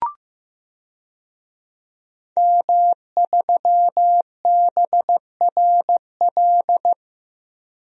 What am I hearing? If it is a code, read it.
Morse code, heard as a keyed tone, about 15 wpm, 700 Hz. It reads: M3BRL
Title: DXCC-100 Call Signs - 5 Characters - Repeated 3x - 15wpm